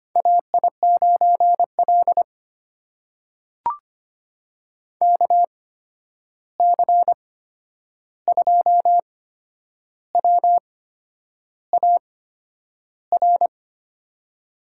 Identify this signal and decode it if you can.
Morse code, heard as a keyed tone, about 25 wpm, spaced as for 9 wpm, 700 Hz. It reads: AI9L KC2WAR
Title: USA Call Signs - 8x Character Spacing - 25wpm